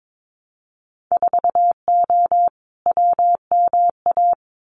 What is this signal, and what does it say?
Morse code, heard as a keyed tone, about 22 wpm, 700 Hz. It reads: 4O WMA
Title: Winter Field Day Exchanges Encore 22wpm